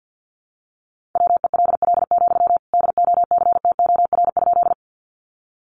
Morse code, heard as a keyed tone, about 50 wpm, 700 Hz. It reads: WELL, DOCTOR?
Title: Sentences from Top 1500 Words 50wpm